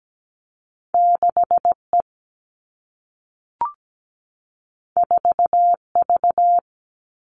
Morse code, heard as a keyed tone, about 17 wpm, 700 Hz. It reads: 6E 4V
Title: All Cut Numbers - 2-Digit 17wpm